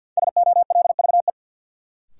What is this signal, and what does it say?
Morse code, heard as a keyed tone, about 47 wpm, 700 Hz. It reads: SOLVE
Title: Sentences from Top 1000 Words - New Word Drill - 50wpm